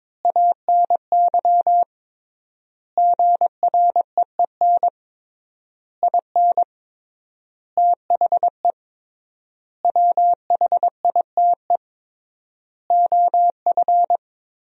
Morse code, heard as a keyed tone, about 21 wpm, 700 Hz. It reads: ANY GREEN IN THE WHITE OF
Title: 3x Word-Spacing - Sentences from Top 500 Words - 22wpm